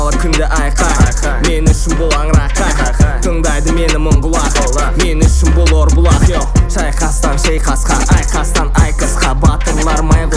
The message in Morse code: none